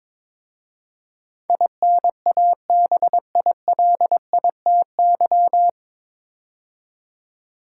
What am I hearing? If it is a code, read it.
Morse code, heard as a keyed tone, about 22 wpm, 700 Hz. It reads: INABILITY
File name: 9 Letter Words - First 3 Letters Given - 22wpm